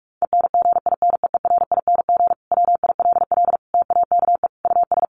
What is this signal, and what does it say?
Morse code, heard as a keyed tone, about 44 wpm, 700 Hz. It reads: ENGINEERING WILL TAKE US
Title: Sentences from Top 1500 Words 45wpm